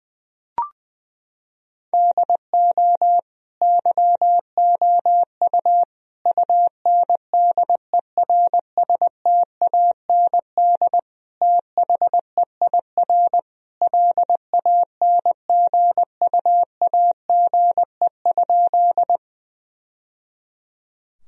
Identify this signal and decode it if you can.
Morse code, heard as a keyed tone, about 20 wpm, 700 Hz. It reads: DO YOU UNDERSTAND THEIR LANGUAGE?
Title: Sentences from Top 1000 Words - Review Part 3 - 20wpm